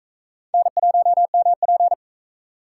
Morse code, heard as a keyed tone, about 42 wpm, 700 Hz. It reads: N1MP